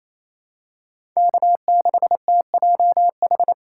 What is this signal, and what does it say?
Morse code, heard as a keyed tone, about 28 wpm, 700 Hz. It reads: K6TJH
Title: USA Call Signs - 5 Characters - 2x Character Spacing - 28wpm